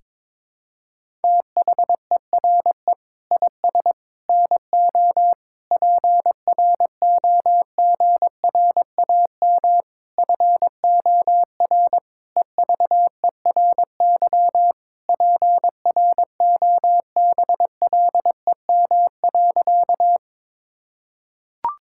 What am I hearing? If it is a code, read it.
Morse code, heard as a keyed tone, about 22 wpm, 700 Hz. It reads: THERE IS NO PROGRAM FOR EVERY PROBLEM.
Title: Sentences from Top 1000 Words - Review Part 5 - 22wpm